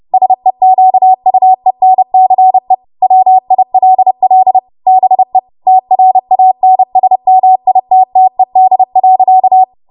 Morse code, heard as a keyed tone, about 30 wpm, 750 Hz. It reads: SEQUENCE WILL BE TRANSMITTED.
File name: Morse / 30WPM